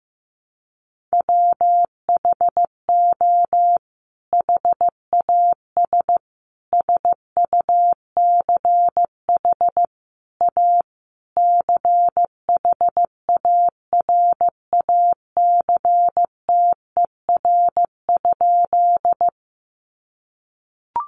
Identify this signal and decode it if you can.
Morse code, heard as a keyed tone, about 15 wpm, 700 Hz. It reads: WHO HAS SUCH A CHARACTER?
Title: Sentences from Top 1200 Words 15wpm